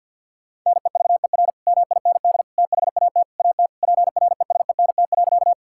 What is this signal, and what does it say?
Morse code, heard as a keyed tone, about 50 wpm, 700 Hz. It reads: NEVER MIND THAT AT PRESENT.
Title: Sentences from Top 1000 Words - Review Part 5 - 50wpm